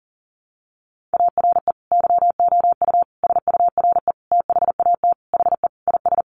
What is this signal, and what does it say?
Morse code, heard as a keyed tone, about 40 wpm, 700 Hz. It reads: ARE YOU SURE THAT HE IS
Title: Sentences from Top 1500 Words 40wpm